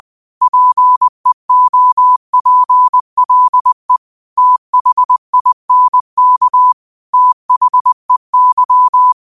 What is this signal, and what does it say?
Morse code, heard as a keyed tone, about 20 wpm, 1000 Hz. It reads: PEOPLE THINK THEY
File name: qotdfiles / 20wpm